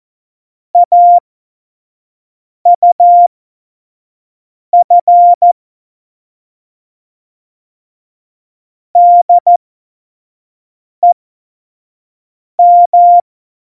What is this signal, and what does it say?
Morse code, heard as a keyed tone, about 14 wpm, spaced as for 5 wpm, 700 Hz. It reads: AUF DEM
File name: Morsekurs / Witze / Witz 02 Juni 2020, 25 F70